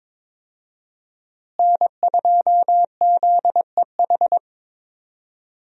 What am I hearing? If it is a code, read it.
Morse code, heard as a keyed tone, about 22 wpm, 700 Hz. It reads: N2ZEH
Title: USA Call Signs - 5 Characters - Repeated 2x - 22wpm